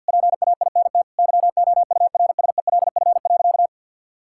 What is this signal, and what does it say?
Morse code, heard as a keyed tone, about 52 wpm, 700 Hz. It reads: PAINT YOURSELF.
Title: Sentences from Top 1000 Words - Review Part 6 - 50wpm